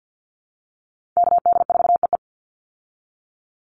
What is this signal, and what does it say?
Morse code, heard as a keyed tone, about 49 wpm, 700 Hz. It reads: KD4EE
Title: Call Signs Repeated 3x - 50wpm